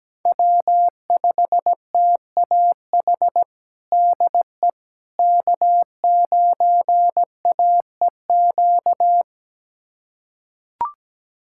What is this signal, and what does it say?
Morse code, heard as a keyed tone, about 17 wpm, 700 Hz. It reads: W5TAH DE K9AEQ
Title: USA Callsign DE Callsign - 5 Characters - 17wpm